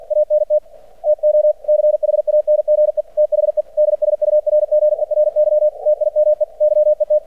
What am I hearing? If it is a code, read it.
Morse code, heard as a keyed tone, about 50 wpm, 600 Hz. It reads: ANT TO CHANGE THE DSURGERY TIME YET
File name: archived / CRSnet